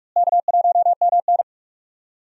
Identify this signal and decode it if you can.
Morse code, heard as a keyed tone, about 46 wpm, 700 Hz. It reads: K1MN
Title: USA Call Signs - 4 Characters - Repeated 3x - 45wpm